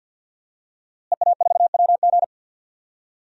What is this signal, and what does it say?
Morse code, heard as a keyed tone, about 51 wpm, 700 Hz. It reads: EA4WG
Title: DXCC-100 Call Signs - 5 Characters - Repeated 3x - 50wpm